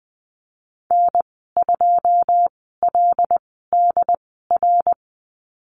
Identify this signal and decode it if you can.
Morse code, heard as a keyed tone, about 20 wpm, spaced as for 14 wpm, 700 Hz. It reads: N2LDR
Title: 088 Lesson 15g - Number 5 - Call Signs 20wpm